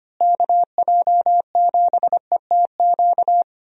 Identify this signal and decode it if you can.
Morse code, heard as a keyed tone, about 25 wpm, 700 Hz. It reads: KJ7ETQ
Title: USA Call Signs - 6 Characters - Repeated 2x - 25wpm